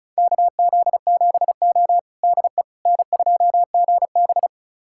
Morse code, heard as a keyed tone, about 36 wpm, 700 Hz. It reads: KZ7O DE N2GB
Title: USA Callsign DE Callsign - 4 Characters - 35wpm